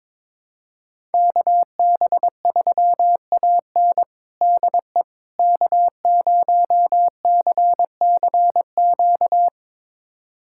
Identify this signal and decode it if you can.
Morse code, heard as a keyed tone, about 22 wpm, 700 Hz. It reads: KB3AN DE K0CCQ
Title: USA Callsign DE Callsign - 5 Characters - 22wpm